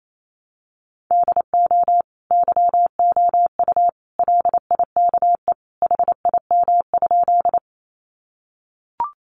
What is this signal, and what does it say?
Morse code, heard as a keyed tone, about 28 wpm, 700 Hz. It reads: DO YOU LIKE HIM?
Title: Sentences from Top 100 Words 28wpm